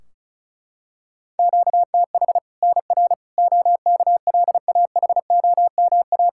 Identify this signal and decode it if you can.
Morse code, heard as a keyed tone, about 36 wpm, 700 Hz. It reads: QTH NR OKLAHOMA
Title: Practice QSO - Exchange 1 - Beginning - 35wpm